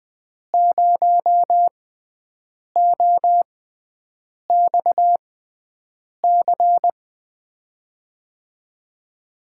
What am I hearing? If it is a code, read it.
Morse code, heard as a keyed tone, about 20 wpm, spaced as for 7 wpm, 700 Hz. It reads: 0OXC
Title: USA Call Signs - 6x Character Spacing - 20wpm